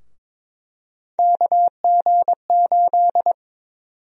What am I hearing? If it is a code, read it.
Morse code, heard as a keyed tone, about 22 wpm, 700 Hz. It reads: KG8
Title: USA Call Sign Prefixes - Repeated 2x - 22wpm